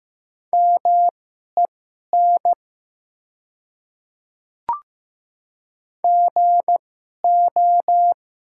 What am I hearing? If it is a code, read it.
Morse code, heard as a keyed tone, about 15 wpm, spaced as for 11 wpm, 700 Hz. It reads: MEN GO
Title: 2x Character Spacing - Top 200 Words - 15wpm